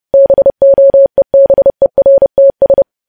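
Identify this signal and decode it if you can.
Morse code, heard as a keyed tone, about 30 wpm, 550 Hz. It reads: BOEBERTS
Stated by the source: Morse code sending of news headlines at 30 wpm.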